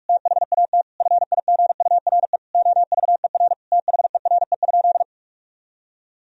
Morse code, heard as a keyed tone, about 41 wpm, 700 Hz. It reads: THAT FIGURE OVER THERE?
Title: Sentences from Top 1000 Words - Review Part 2 - 45wpm